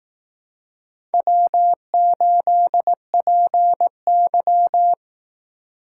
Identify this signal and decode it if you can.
Morse code, heard as a keyed tone, about 18 wpm, 700 Hz. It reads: W8PY
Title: Speed Racing - USA 1x2 Call Signs 15wpm